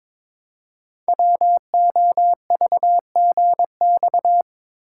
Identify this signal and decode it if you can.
Morse code, heard as a keyed tone, about 22 wpm, 700 Hz. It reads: WOVGX
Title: Code Groups - 5 Letters - 22wpm